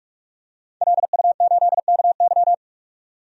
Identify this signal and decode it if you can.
Morse code, heard as a keyed tone, about 43 wpm, 700 Hz. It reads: LU8KY
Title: DXCC-100 Call Signs - 5 Characters - 45wpm